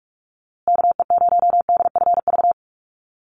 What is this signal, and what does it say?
Morse code, heard as a keyed tone, about 44 wpm, 700 Hz. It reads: KE0DFV
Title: USA Call Signs Repeated 3x 45wpm